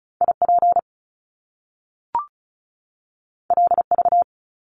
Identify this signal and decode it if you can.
Morse code, heard as a keyed tone, about 36 wpm, 700 Hz. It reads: IP LV
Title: Top Two Letter Combinations in Words 35wpm